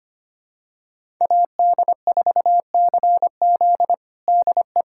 Morse code, heard as a keyed tone, about 25 wpm, 700 Hz. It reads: AD4CZ DE
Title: USA Callsign DE Callsign - 5 Characters - 25wpm